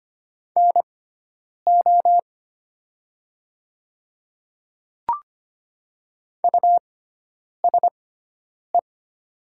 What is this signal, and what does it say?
Morse code, heard as a keyed tone, about 25 wpm, spaced as for 9 wpm, 700 Hz. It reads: NO USE
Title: 6x Character Spacing - Top 100 Words - 25wpm